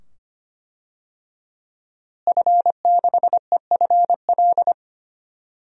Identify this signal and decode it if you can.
Morse code, heard as a keyed tone, about 25 wpm, 700 Hz. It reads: F6EFL
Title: DXCC-100 Call Signs - 5 Characters - Repeated 2x - 25wpm